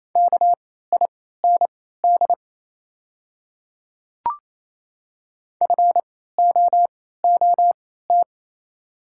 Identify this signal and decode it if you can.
Morse code, heard as a keyed tone, about 28 wpm, spaced as for 16 wpm, 690 Hz. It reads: KIND FOOT
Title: 3x Character Spacing - Top 500 Words - 28wpm